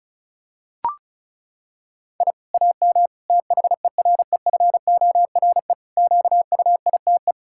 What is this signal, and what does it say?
Morse code, heard as a keyed tone, about 36 wpm, 700 Hz. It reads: I AM THEREFORE QUITE
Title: Sentences from Top 1000 Words - Review Part 6 - 35wpm